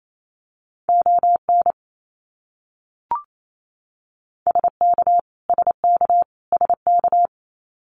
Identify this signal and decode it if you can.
Morse code, heard as a keyed tone, about 28 wpm, 700 Hz. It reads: ON SK SK SK
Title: Canadian Provinces and Territories - Repeated 3x - 28wpm